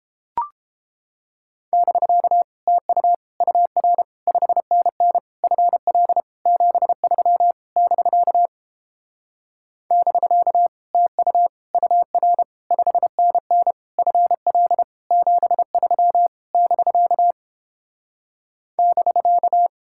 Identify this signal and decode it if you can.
Morse code, heard as a keyed tone, about 29 wpm, 700 Hz. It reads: <BK> TU UR 5NN FL 73 <BK> <BK> TU UR 5NN FL 73 <BK> <BK>